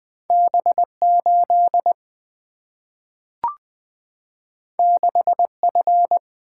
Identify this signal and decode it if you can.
Morse code, heard as a keyed tone, about 20 wpm, 700 Hz. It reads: B8 6F
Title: Single-Letter Single-Number 20wpm